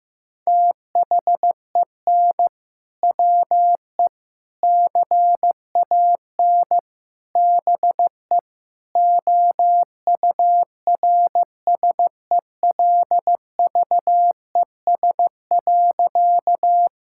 Morse code, heard as a keyed tone, about 16 wpm, 700 Hz. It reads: THEN WE CAN BE OURSELVES.
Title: Sentences from Top 1600 Words 15wpm